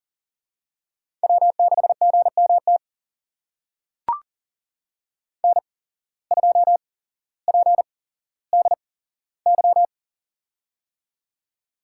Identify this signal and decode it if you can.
Morse code, heard as a keyed tone, about 40 wpm, spaced as for 17 wpm, 700 Hz. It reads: W6GMT N2PDY
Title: USA Call Signs - 5 Characters - 8x Character Spacing - 40wpm